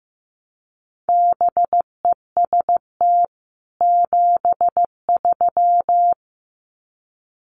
Morse code, heard as a keyed tone, about 15 wpm, 700 Hz. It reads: BEST 73
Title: Common QSO Phrases 15wpm